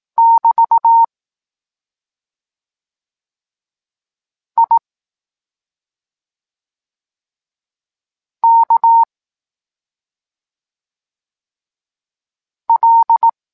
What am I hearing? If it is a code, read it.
Morse code, heard as a keyed tone, about 18 wpm, 920 Hz. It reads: <BT> I K L
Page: Tekst 23 – Doskonalenie odbioru poznanych liter